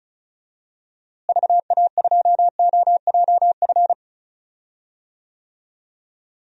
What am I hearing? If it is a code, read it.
Morse code, heard as a keyed tone, about 36 wpm, 700 Hz. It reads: VA2OJF
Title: DXCC-100 Call Signs - 6 Characters - Repeated 3x - 35wpm